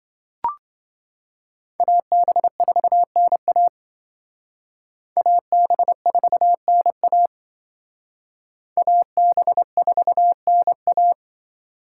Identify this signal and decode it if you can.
Morse code, heard as a keyed tone, about 25 wpm, 700 Hz. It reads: AB4NA AB4NA AB4NA